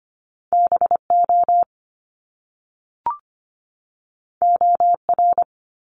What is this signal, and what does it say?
Morse code, heard as a keyed tone, about 25 wpm, 700 Hz. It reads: BO OR
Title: Top Two Letter Combinations in Words 25wpm